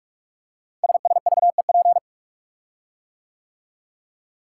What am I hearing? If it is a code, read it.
Morse code, heard as a keyed tone, about 44 wpm, 700 Hz. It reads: SSVEP
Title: Code Groups - 5 Letters - 45wpm